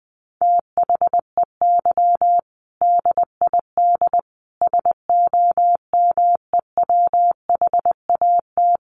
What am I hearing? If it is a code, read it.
Morse code, heard as a keyed tone, about 20 wpm, 700 Hz. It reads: THEY DID SOMEWHAT
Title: Sentences from Top 1300 Words 20wpm